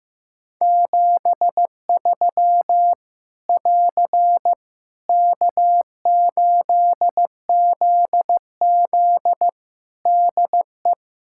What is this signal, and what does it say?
Morse code, heard as a keyed tone, about 15 wpm, 700 Hz. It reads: 73 <AR> K8ZZ DE
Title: K1USN SST QSO Practice 15wpm